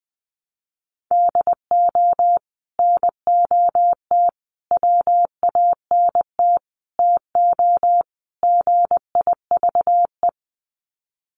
Morse code, heard as a keyed tone, about 20 wpm, 700 Hz. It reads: DO NOT WANT TO GIVE
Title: Sets of 5 Words Encore 20wpm